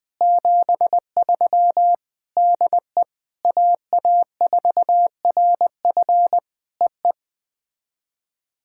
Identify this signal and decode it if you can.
Morse code, heard as a keyed tone, about 19 wpm, 700 Hz. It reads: 73 DE AA4RF EE
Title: POTA QSO Practice - 20wpm